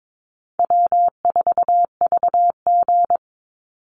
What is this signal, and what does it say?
Morse code, heard as a keyed tone, about 22 wpm, 700 Hz. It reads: W4VG
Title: Easy Call Signs - 22wpm